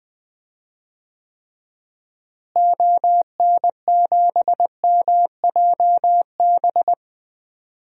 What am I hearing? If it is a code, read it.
Morse code, heard as a keyed tone, about 20 wpm, 700 Hz. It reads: ON7MJB
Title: DXCC-100 Call Signs - 6 Characters - Repeated 3x - 20wpm